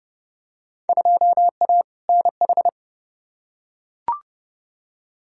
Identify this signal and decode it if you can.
Morse code, heard as a keyed tone, about 31 wpm, 700 Hz. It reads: 2A NH
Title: Field Day Exchanges Encore - 30wpm